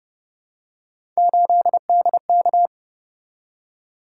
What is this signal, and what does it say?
Morse code, heard as a keyed tone, about 30 wpm, 700 Hz. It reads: 8DK
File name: Speed-Racing - Three Letter-Number Combinations - 25wpm